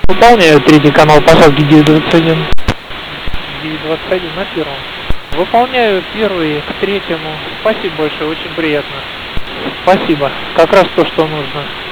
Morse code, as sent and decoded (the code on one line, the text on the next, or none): none